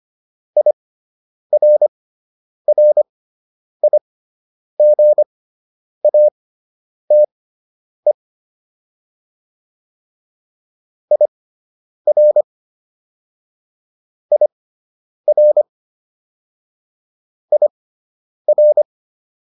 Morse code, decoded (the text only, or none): IRRIGATE IR IR IR